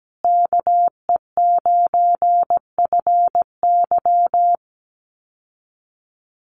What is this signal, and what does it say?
Morse code, heard as a keyed tone, about 17 wpm, 700 Hz. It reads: KE9FY
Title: USA Call Signs Repeated 2x 17wpm